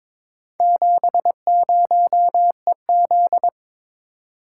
Morse code, heard as a keyed tone, about 22 wpm, 700 Hz. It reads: 70EZ
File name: practice / 22wpm / Four Letter-Number Combinations - 22wpm